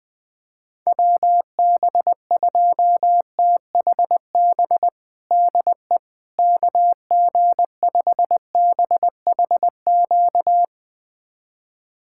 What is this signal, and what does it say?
Morse code, heard as a keyed tone, about 20 wpm, 700 Hz. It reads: WB2THB DE KG5BHQ
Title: USA Callsign DE Callsign 20wpm